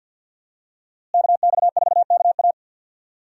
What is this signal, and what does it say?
Morse code, heard as a keyed tone, about 53 wpm, 700 Hz. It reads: KX4KA
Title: USA Call Signs - 5 Characters - Repeated 2x - 50wpm